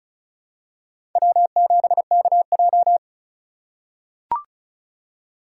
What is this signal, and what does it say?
Morse code, heard as a keyed tone, about 34 wpm, 700 Hz. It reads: W7KJ
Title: USA Call Signs - 4 Characters - 4x Character Spacing - 35wpm